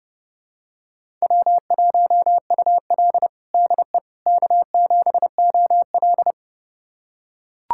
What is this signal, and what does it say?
Morse code, heard as a keyed tone, about 30 wpm, 700 Hz. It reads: W1UL DE K7OL
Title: USA Callsign DE Callsign - 4 Characters - 30wpm